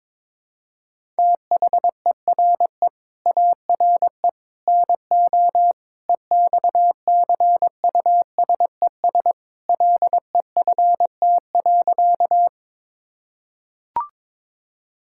Morse code, heard as a keyed tone, about 22 wpm, 700 Hz. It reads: THERE ARE NO EXCUSES LEFT.
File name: practice / 22wpm / Sentences with 3-Word Phrases - 22wpm